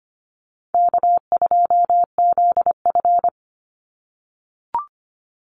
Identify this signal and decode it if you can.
Morse code, heard as a keyed tone, about 25 wpm, 700 Hz. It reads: K2ZF
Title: Easy Call Signs - 25wpm